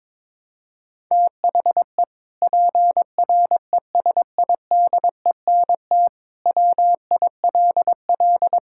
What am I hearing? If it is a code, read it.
Morse code, heard as a keyed tone, about 22 wpm, 700 Hz. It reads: THE PRESIDENT WILL
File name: practice / 22wpm / Sentences with 2-Word Phrases - 22wpm